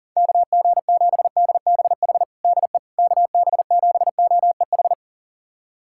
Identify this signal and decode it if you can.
Morse code, heard as a keyed tone, about 40 wpm, 690 Hz. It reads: KG7DBH DE KB7OEH
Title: USA Callsign DE Callsign - 6 Characters - 40wpm